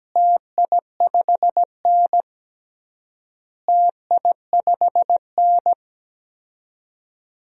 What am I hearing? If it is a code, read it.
Morse code, heard as a keyed tone, about 17 wpm, 700 Hz. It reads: TI5N TI5N
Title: DXCC-100 Call Signs - 4 Characters - Repeated 3x - 17wpm